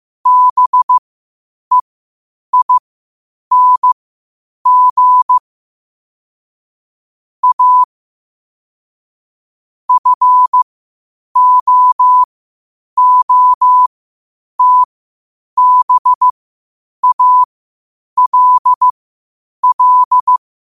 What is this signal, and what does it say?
Morse code, heard as a keyed tone, about 15 wpm, spaced as for 8 wpm, 1000 Hz. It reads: BEING A FOOTBALL
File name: qotdfiles / 8wpm